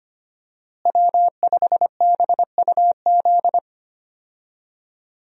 Morse code, heard as a keyed tone, about 25 wpm, 700 Hz. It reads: W5BUZ
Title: USA Call Signs - 5 Characters - Repeated 2x - 25wpm